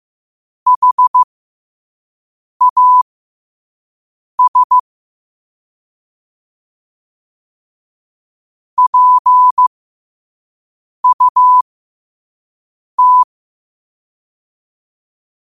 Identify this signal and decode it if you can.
Morse code, heard as a keyed tone, about 15 wpm, spaced as for 5 wpm, 1000 Hz. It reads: HAS PUT
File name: qotdfiles / 5wpm